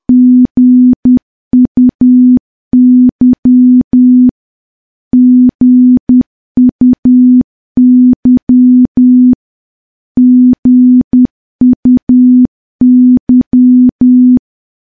Morse code, heard as a keyed tone, about 10 wpm, 250 Hz. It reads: GUY GUY GUY
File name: morse / morse-training-10WPM